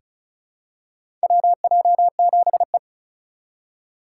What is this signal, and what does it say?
Morse code, heard as a keyed tone, about 34 wpm, 700 Hz. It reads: WJ7E
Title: USA Call Signs - 4 Characters - 3x Character Spacing - 35wpm